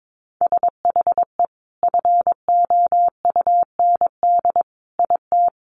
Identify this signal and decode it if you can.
Morse code, heard as a keyed tone, about 22 wpm, 700 Hz. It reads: SHE FOUND IT
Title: Sentences from Top 1500 Words 22wpm